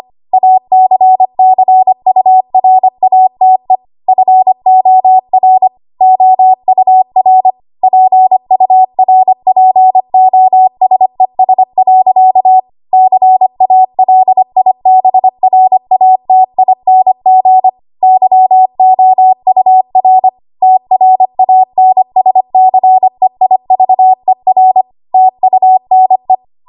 Morse code, heard as a keyed tone, about 25 wpm, 750 Hz. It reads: ACCURATE FOR OUR PURPOSES. CALIBRATING YOUR TRANSCEIVER TUNE
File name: Archive / 25 WPM / 25WPM